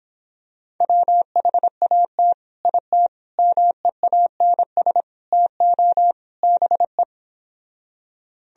Morse code, heard as a keyed tone, about 26 wpm, 700 Hz. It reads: WHAT IT MEANS TO BE